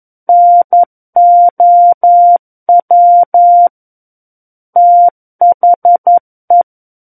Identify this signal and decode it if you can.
Morse code, heard as a keyed tone, about 11 wpm, 700 Hz. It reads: NOW THE
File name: Book One / 02-Chapter 2 - 11 WPM